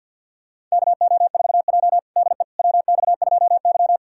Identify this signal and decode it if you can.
Morse code, heard as a keyed tone, about 48 wpm, 700 Hz. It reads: KO4J DE WX2Y
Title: USA Callsign DE Callsign - 4 Characters - 50wpm